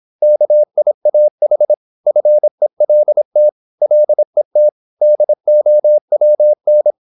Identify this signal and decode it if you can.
Morse code, heard as a keyed tone, about 26 wpm, 600 Hz. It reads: KIAH FELT LET DOWN
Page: Never 26wpm | CW med Gnister